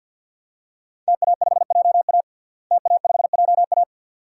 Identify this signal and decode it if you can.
Morse code, heard as a keyed tone, about 46 wpm, 700 Hz. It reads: TA5JA TA5JA
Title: DXCC-100 Call Signs - 5 Characters - Repeated 2x - 50wpm